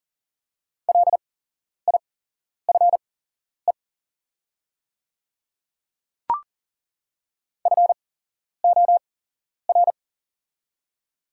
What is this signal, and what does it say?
Morse code, heard as a keyed tone, about 40 wpm, spaced as for 11 wpm, 700 Hz. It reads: LIFE FOR
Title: 8x Character Spacing - Top 500 Words - 40wpm